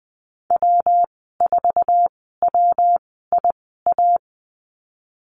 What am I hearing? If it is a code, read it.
Morse code, heard as a keyed tone, about 20 wpm, spaced as for 15 wpm, 700 Hz. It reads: W4WIA
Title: 147 Lesson 24g - Letter y - Call Signs 20wpm